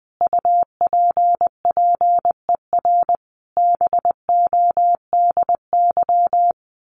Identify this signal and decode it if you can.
Morse code, heard as a keyed tone, about 20 wpm, 700 Hz. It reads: UPPER BODY
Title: A Sound Of Thunder 20wpm